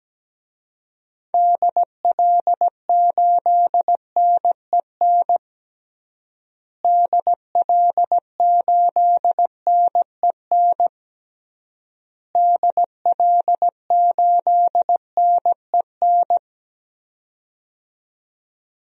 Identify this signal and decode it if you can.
Morse code, heard as a keyed tone, about 17 wpm, 700 Hz. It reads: DL8NEN DL8NEN DL8NEN
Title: DXCC-100 Call Signs - 6 Characters - Repeated 3x - 17wpm